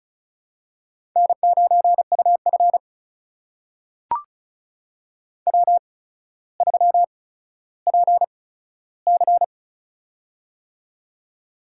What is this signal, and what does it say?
Morse code, heard as a keyed tone, about 36 wpm, spaced as for 15 wpm, 700 Hz. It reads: N9UF W3PC
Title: USA Call Signs - 8x Character Spacing - 35wpm